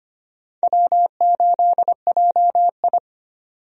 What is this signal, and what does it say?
Morse code, heard as a keyed tone, about 25 wpm, 700 Hz. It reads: W8JI
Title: USA Call Signs - 4 Characters - 25wpm